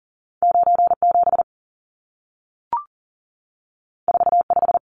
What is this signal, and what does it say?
Morse code, heard as a keyed tone, about 40 wpm, 700 Hz. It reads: 97 45
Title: Numbers - 2-Digit 40wpm